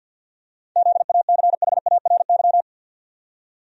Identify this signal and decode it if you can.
Morse code, heard as a keyed tone, about 52 wpm, 700 Hz. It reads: ZACHARY
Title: USA Common Names - 50wpm